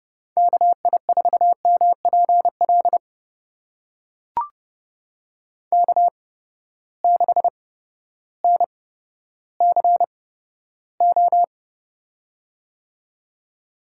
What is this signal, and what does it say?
Morse code, heard as a keyed tone, about 31 wpm, spaced as for 14 wpm, 700 Hz. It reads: KI4MPL K6NCO
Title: USA Call Signs - 8x Character Spacing - 30wpm